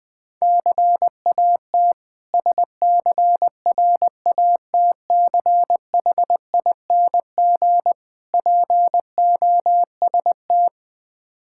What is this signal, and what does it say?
Morse code, heard as a keyed tone, about 20 wpm, 700 Hz. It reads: CAT SCRATCHING POST
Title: Difficult Words with Hints - 20wpm